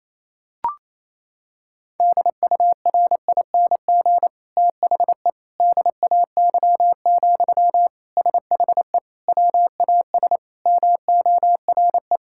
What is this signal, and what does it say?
Morse code, heard as a keyed tone, about 28 wpm, 700 Hz. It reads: DURING THE DAY, SHE WAS MORE
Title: Sentences from Top 1000 Words - Review Part 6 - 28wpm